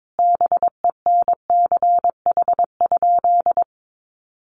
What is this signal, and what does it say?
Morse code, heard as a keyed tone, about 22 wpm, 700 Hz. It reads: BENCH?
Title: Sentences from Top 1400 Words 22wpm